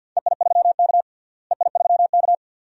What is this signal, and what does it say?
Morse code, heard as a keyed tone, about 52 wpm, 700 Hz. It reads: EI3K EI3K
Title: DXCC-100 Call Signs - 4 Characters - Repeated 2x - 50wpm